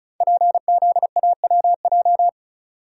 Morse code, heard as a keyed tone, about 35 wpm, 700 Hz. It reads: PZAWJ
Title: Code Groups - 5 Letter:Numbers - 35wpm